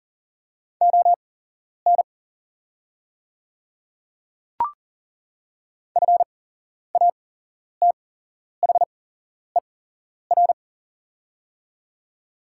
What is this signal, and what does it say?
Morse code, heard as a keyed tone, about 40 wpm, spaced as for 11 wpm, 700 Hz. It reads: ON FATHER
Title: 8x Character Spacing - Top 300 Words - 40wpm